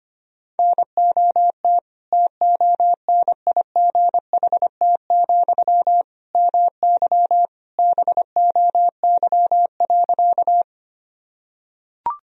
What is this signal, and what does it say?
Morse code, heard as a keyed tone, about 25 wpm, 700 Hz. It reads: NOT TONIGHT, MY BOY.
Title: Sentences from Top 1000 Words - Review Part 3 - 25wpm